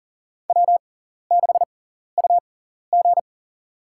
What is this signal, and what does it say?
Morse code, heard as a keyed tone, about 40 wpm, spaced as for 14 wpm, 700 Hz. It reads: W6UG
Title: USA Call Signs - 6x Character Spacing - 40wpm